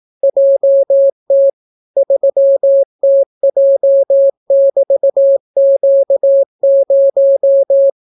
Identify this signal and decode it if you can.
Morse code, heard as a keyed tone, about 18 wpm, 550 Hz. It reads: JT 3TJ<BT>Q0